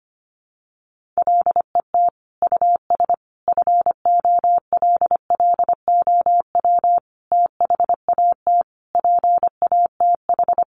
Morse code, heard as a keyed tone, about 25 wpm, 700 Hz. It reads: LET US FOLLOW THAT PATH
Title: Sentences from Top 1400 Words 25wpm